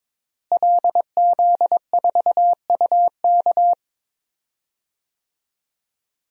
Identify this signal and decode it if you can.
Morse code, heard as a keyed tone, about 22 wpm, 700 Hz. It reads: LZ4UK
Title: DXCC-100 Call Signs - 5 Characters - Repeated 3x - 22wpm